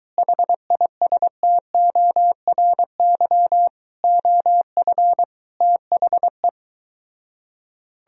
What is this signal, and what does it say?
Morse code, heard as a keyed tone, about 23 wpm, 700 Hz. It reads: HISTORY OF THE